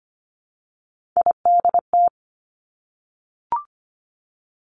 Morse code, heard as a keyed tone, about 25 wpm, 700 Hz. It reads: IDT
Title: Three Challenging Letter-Number Combinations 25wpm